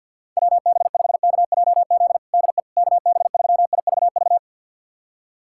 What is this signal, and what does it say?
Morse code, heard as a keyed tone, about 51 wpm, 700 Hz. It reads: WB5KJZ DE KB3IVV
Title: USA Callsign DE Callsign - 6 Characters - 50wpm